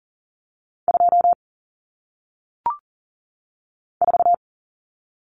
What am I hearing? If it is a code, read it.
Morse code, heard as a keyed tone, about 39 wpm, 700 Hz. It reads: 2 4
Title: Numbers - 1-Digit 40wpm